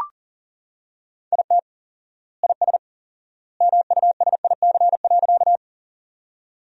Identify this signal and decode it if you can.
Morse code, heard as a keyed tone, about 40 wpm, 700 Hz. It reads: IT IS MUSIC.
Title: 4x Word-Spacing - Sentences from Top 300 Words - 40wpm